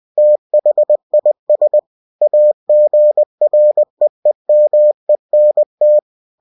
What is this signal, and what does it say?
Morse code, heard as a keyed tone, about 20 wpm, 600 Hz. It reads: THIS AGREEMENT
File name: cw-books / winnie-the-pooh / winnie-20wpm-page-0117